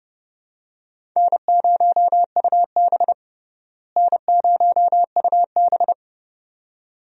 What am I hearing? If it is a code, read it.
Morse code, heard as a keyed tone, about 30 wpm, 700 Hz. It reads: N0UB N0UB
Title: USA Call Signs - 4 Characters - Repeated 2x - 30wpm